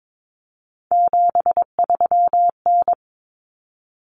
Morse code, heard as a keyed tone, about 22 wpm, 700 Hz. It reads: 73N
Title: Common Cut Numbers - 3-Digit 22wpm